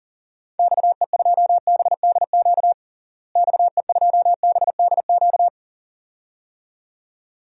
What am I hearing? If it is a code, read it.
Morse code, heard as a keyed tone, about 40 wpm, 700 Hz. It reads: XE2BDQ XE2BDQ
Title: DXCC-100 Call Signs - 6 Characters - Repeated 2x - 40wpm